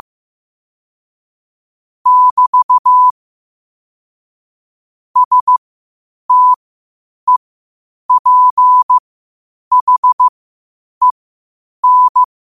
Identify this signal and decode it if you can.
Morse code, heard as a keyed tone, about 15 wpm, spaced as for 8 wpm, 1000 Hz. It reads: <BT> STEPHEN
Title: 8 WPM morse code quotes for Fri, 01 Mar 2024